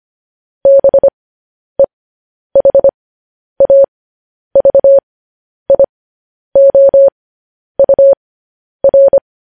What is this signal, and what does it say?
Morse code, heard as a keyed tone, about 25 wpm, spaced as for 10 wpm, 550 Hz. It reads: BEHAVIOUR
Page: news headlines in morse code at 10 wpm